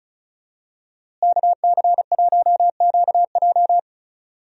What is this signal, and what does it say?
Morse code, heard as a keyed tone, about 36 wpm, 700 Hz. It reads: KC1QJ
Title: USA Call Signs - 5 Characters - Repeated 2x - 35wpm